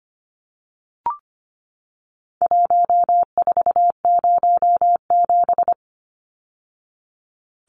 Morse code, heard as a keyed tone, about 25 wpm, 700 Hz. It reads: 1407
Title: Numbers - Contest Style - 4-Digit 25wpm